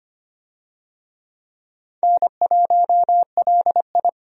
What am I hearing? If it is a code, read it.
Morse code, heard as a keyed tone, about 25 wpm, 700 Hz. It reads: N1LI
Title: USA Call Signs - 4 Characters - 25wpm